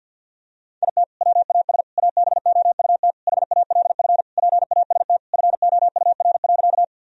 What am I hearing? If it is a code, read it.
Morse code, heard as a keyed tone, about 50 wpm, 700 Hz. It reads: IT WAS ABOUT HALF PAST FOUR.
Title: Sentences from Top 1000 Words - Review Part 3 - 50wpm